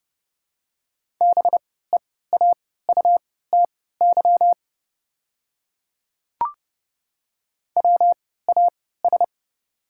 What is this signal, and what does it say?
Morse code, heard as a keyed tone, about 32 wpm, spaced as for 17 wpm, 700 Hz. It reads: BEAUTY WAS
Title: 3x Character Spacing - Top 500 Words - 30wpm